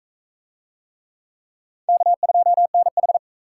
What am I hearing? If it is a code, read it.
Morse code, heard as a keyed tone, about 41 wpm, 700 Hz. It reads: K2NH